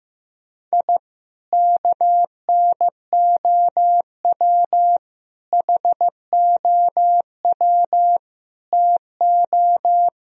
Morse code, heard as a keyed tone, about 15 wpm, 700 Hz. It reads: I KNOW HOW TO
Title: Sentences from Top 1000 Words - Review Part 2 - 15wpm